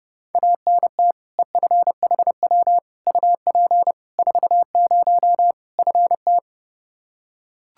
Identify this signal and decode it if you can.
Morse code, heard as a keyed tone, about 32 wpm, 700 Hz. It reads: ANT EFHW UP 40 FT
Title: K1USN SST QSO Practice 30wpm